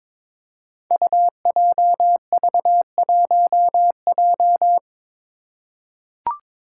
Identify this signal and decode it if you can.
Morse code, heard as a keyed tone, about 22 wpm, 700 Hz. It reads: UJV1J
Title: Five Challenging Letter-Number Combinations 22wpm